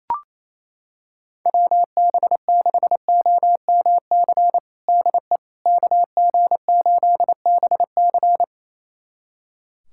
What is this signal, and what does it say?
Morse code, heard as a keyed tone, about 28 wpm, 700 Hz. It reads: WB6OMC DE KG8BC
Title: USA Callsign DE Callsign 28wpm